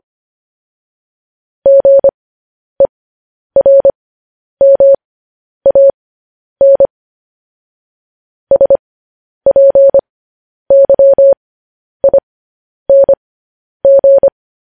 Morse code, heard as a keyed tone, about 25 wpm, spaced as for 10 wpm, 550 Hz. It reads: GERMAN SPYING